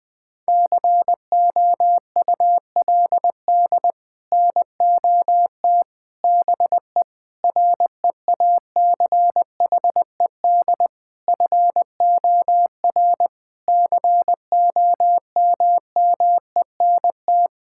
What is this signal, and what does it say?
Morse code, heard as a keyed tone, about 21 wpm, 700 Hz. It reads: COULD NOT BE REACHED FOR COMMENT
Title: Sets of 6 Words - 20wpm